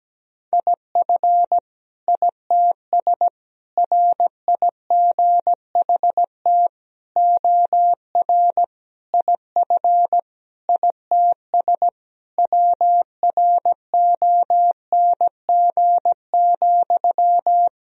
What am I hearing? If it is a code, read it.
Morse code, heard as a keyed tone, about 17 wpm, 700 Hz. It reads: IF ITS RIGHT OR IF ITS WRONG,
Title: Sentences from Top 1000 Words - Review Part 3 - 17wpm